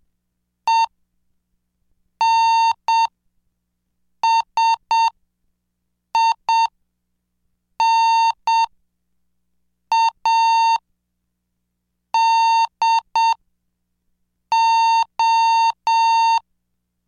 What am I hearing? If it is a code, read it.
Morse code, heard as a keyed tone, about 7 wpm, spaced as for 5 wpm, 910 Hz. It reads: ENSINADO
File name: cursos / CDestudo / 26 Track 26